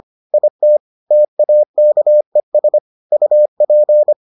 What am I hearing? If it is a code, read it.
Morse code, heard as a keyed tone, about 25 wpm, 600 Hz. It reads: IT TAKES UP